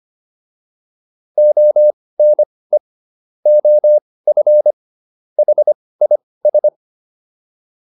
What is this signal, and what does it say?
Morse code, heard as a keyed tone, about 25 wpm, spaced as for 18 wpm, 600 Hz. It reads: ONE OF HIS